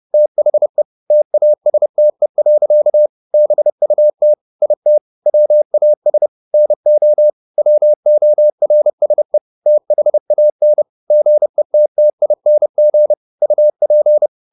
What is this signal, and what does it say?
Morse code, heard as a keyed tone, about 30 wpm, 600 Hz. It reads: THE TASTE. BUT IT WAS NO WORSE THAN GETTING UP
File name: the old man and the sea / the old man and the sea 1 30wpm